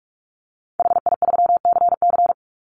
Morse code, heard as a keyed tone, about 44 wpm, 700 Hz. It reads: HI3CC
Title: Call Signs - 45wpm